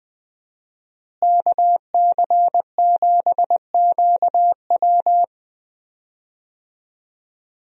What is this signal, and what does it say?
Morse code, heard as a keyed tone, about 20 wpm, 700 Hz. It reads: KC7QW
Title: USA Call Signs - 5 Characters - Repeated 3x - 20wpm